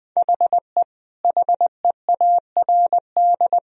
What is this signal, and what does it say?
Morse code, heard as a keyed tone, about 20 wpm, 700 Hz. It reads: HE HEARD
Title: Sentences from Top 1000 Words - Review Part 3 - 20wpm